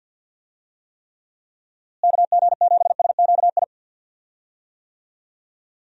Morse code, heard as a keyed tone, about 47 wpm, 700 Hz. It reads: KG7SQI
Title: USA Call Signs - 6 Characters - Repeated 3x - 50wpm